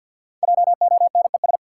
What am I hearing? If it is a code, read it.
Morse code, heard as a keyed tone, about 53 wpm, 700 Hz. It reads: JONES
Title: Sets of 5 Words - New Word Drill - 50wpm